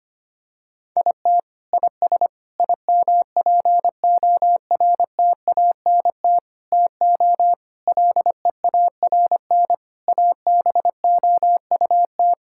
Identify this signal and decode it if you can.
Morse code, heard as a keyed tone, about 25 wpm, 700 Hz. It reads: IT IS IMPORTANT TO LEARN ABOUT
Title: Sentences from Top 1000 Words - Review Part 6 - 25wpm